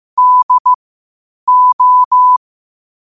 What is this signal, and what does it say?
Morse code, heard as a keyed tone, about 15 wpm, spaced as for 8 wpm, 1000 Hz. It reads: DO
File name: qotdfiles / 8wpm